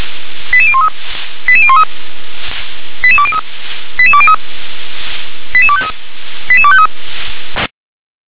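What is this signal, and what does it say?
Five-tone selective call (ZVEI), heard as quick runs of five tones, tones of about 70 ms. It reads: 88013 88013 88203 88203 88253 88253
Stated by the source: ZVEI five-tone selective call 88161 received with an SDR.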